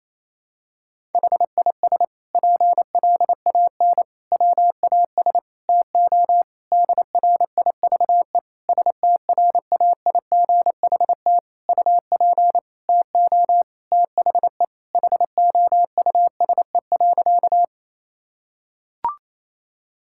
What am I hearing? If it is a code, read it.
Morse code, heard as a keyed tone, about 28 wpm, 700 Hz. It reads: HIS PLAN WAS TO DRIVE STRAIGHT UP TO THE HOUSE.
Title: Sentences from Top 1000 Words - Review Part 6 - 28wpm